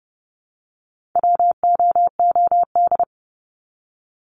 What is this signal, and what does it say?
Morse code, heard as a keyed tone, about 30 wpm, 700 Hz. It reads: WOOD
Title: Top 300 Words 30wpm